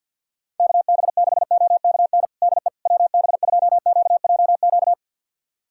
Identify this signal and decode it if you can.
Morse code, heard as a keyed tone, about 47 wpm, 700 Hz. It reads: KB6OKN DE WB2QJQ
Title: USA Callsign DE Callsign - 6 Characters - 50wpm